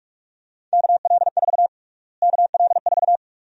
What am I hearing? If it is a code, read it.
Morse code, heard as a keyed tone, about 45 wpm, 700 Hz. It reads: KL4 KL4
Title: USA Call Sign Prefixes - Repeated 2x - 45wpm